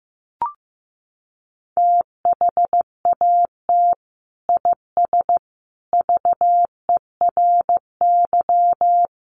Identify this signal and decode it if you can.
Morse code, heard as a keyed tone, about 15 wpm, 700 Hz. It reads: THAT IS VERY
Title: Sentences from Top 1100 Words 15wpm